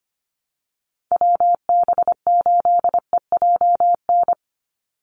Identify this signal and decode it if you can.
Morse code, heard as a keyed tone, about 25 wpm, 700 Hz. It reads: WB8EJN
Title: Call Signs Repeated 2x - 25wpm